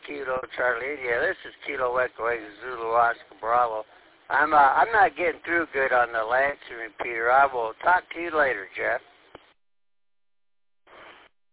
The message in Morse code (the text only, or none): none